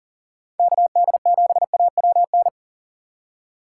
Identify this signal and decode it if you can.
Morse code, heard as a keyed tone, about 40 wpm, 700 Hz. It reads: KD7AWN
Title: USA Call Signs - 6 Characters - 6x Character Spacing - 40wpm